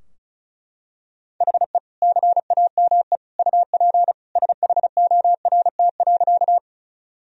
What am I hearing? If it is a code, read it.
Morse code, heard as a keyed tone, about 34 wpm, 700 Hz. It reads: HE CAME UP SHORT.
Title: Sentences from Top 1000 Words - Review Part 2 - 35wpm